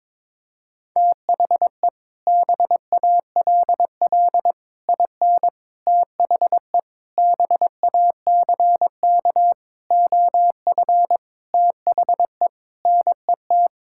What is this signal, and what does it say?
Morse code, heard as a keyed tone, about 22 wpm, 700 Hz. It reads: THE BALL IN THE BACK OF THE NET
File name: practice / 22wpm / Sets of 8 Words - 22wpm